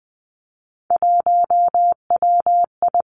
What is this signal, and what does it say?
Morse code, heard as a keyed tone, about 20 wpm, 700 Hz. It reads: 1WI
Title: Three Challenging Letter-Number Combinations 20wpm